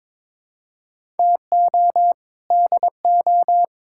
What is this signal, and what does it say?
Morse code, heard as a keyed tone, about 22 wpm, 700 Hz. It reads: TO DO
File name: practice / 22wpm / Sets of 6 Words - 22wpm